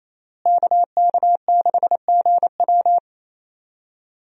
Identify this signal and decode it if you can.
Morse code, heard as a keyed tone, about 27 wpm, 700 Hz. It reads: KK6GW
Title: USA Call Signs - 5 Characters - Repeated 3x - 28wpm